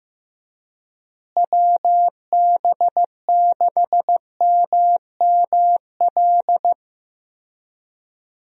Morse code, heard as a keyed tone, about 15 wpm, 700 Hz. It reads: WB6MML
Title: USA Call Signs - 6 Characters - Repeated 3x - 15wpm